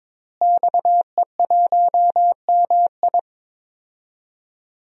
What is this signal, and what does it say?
Morse code, heard as a keyed tone, about 22 wpm, 700 Hz. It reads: XE1MI
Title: DXCC-100 Call Signs - 5 Characters - Repeated 2x - 22wpm